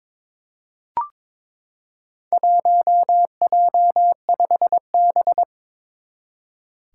Morse code, heard as a keyed tone, about 22 wpm, 700 Hz. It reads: 1J5B
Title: Four Challenging Letter-Number Combinations 22wpm